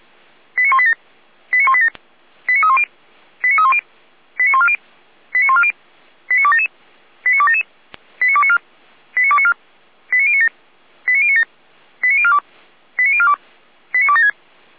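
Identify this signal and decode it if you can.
Five-tone selective call (ZVEI), heard as quick runs of five tones, tones of about 80 ms. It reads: 89187 89187 89310 89310 89140 89140 89277 89277 89294 89294 89097 89097 89042 89042 89276
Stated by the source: ZVEI five-tone selective call received on SDR.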